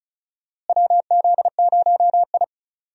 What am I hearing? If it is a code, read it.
Morse code, heard as a keyed tone, about 36 wpm, 700 Hz. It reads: WZ0I
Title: USA Call Signs - 4 Characters - Repeated 3x - 35wpm